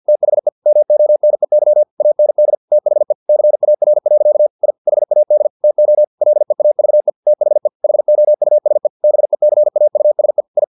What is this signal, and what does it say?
Morse code, heard as a keyed tone, about 50 wpm, 600 Hz. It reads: THE MONEY AND THE CAR. I HAD TO LEAVE THE HOUSE BECAUSE I
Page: Never 49wpm